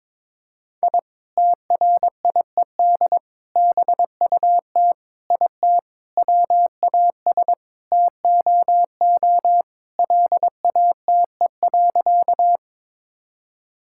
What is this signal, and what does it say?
Morse code, heard as a keyed tone, about 22 wpm, 700 Hz. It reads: I TRIED BUT IT WAS TOO LATE.
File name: practice / 22wpm / Sentences with 3-Word Phrases - 22wpm